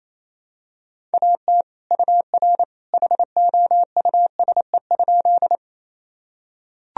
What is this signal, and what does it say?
Morse code, heard as a keyed tone, about 28 wpm, 700 Hz. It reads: AT UR HOUSE?
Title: Sentences from Top 200 Words and Abbreviations 28wpm